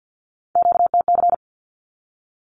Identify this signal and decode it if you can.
Morse code, heard as a keyed tone, about 49 wpm, 700 Hz. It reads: QTC
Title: Q-Codes 50wpm